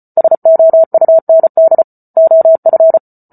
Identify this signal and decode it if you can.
Morse code, heard as a keyed tone, about 34 wpm, 650 Hz. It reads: SOUND OF